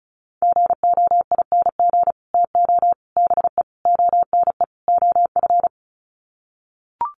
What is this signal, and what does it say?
Morse code, heard as a keyed tone, about 34 wpm, 700 Hz. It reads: GOING TO BE ONE OF
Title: Sets of 5 Words Encore2 35wpm